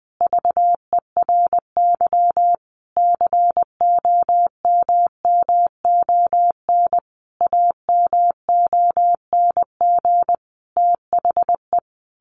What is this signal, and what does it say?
Morse code, heard as a keyed tone, about 20 wpm, 700 Hz. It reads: VERY COMMON AMONG THE
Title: Sentences from Top 1400 Words 20wpm